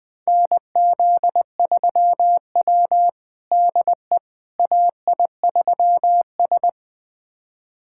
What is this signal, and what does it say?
Morse code, heard as a keyed tone, about 20 wpm, 700 Hz. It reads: NZ3W DE AI3S
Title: USA Callsign DE Callsign - 4 Characters - 20wpm